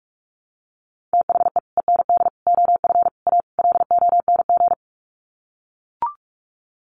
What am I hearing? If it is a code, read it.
Morse code, heard as a keyed tone, about 44 wpm, 700 Hz. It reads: THE END OF A LONG
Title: Sets of 5 Words Encore2 45wpm